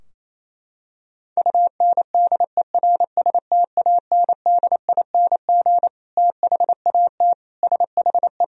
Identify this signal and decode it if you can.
Morse code, heard as a keyed tone, about 28 wpm, 700 Hz. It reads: UNDERSTANDING THAT SHE
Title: Sentences from Top 1000 Words - Review Part 3 - 28wpm